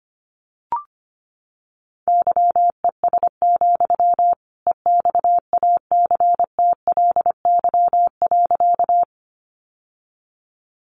Morse code, heard as a keyed tone, about 25 wpm, 700 Hz. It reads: YES, EXACTLY.
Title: Sentences from Top 900 Words 25wpm